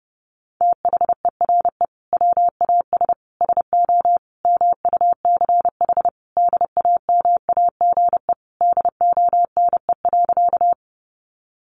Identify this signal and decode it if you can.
Morse code, heard as a keyed tone, about 30 wpm, 700 Hz. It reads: THERE WAS SO MUCH DAMAGE DONE.
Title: Sentences from Top 1100 Words 30wpm